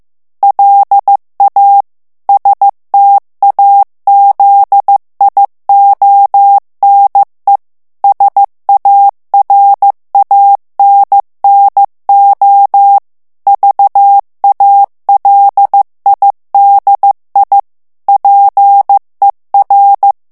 Morse code, heard as a keyed tone, about 15 wpm, 800 Hz. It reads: LA STAZIONE SARANNO VALIDI PER
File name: ITALIEN texte clair 1 vitesse 14 mots minute